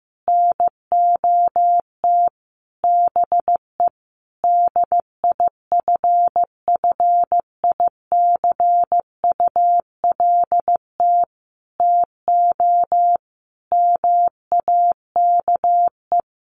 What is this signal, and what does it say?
Morse code, heard as a keyed tone, about 15 wpm, 700 Hz. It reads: NOT BE DIFFICULT TO MAKE
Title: Sentences from Top 600 Words 15wpm